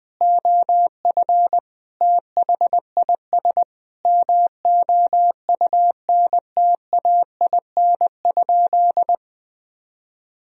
Morse code, heard as a keyed tone, about 20 wpm, 700 Hz. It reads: OF THIS MOUNTAIN?
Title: Sentences from Top 1000 Words - Review Part 1 - 20wpm